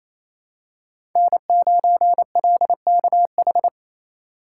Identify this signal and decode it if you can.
Morse code, heard as a keyed tone, about 28 wpm, 700 Hz. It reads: N9LKH
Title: USA Call Signs - 8x Character Spacing - 28wpm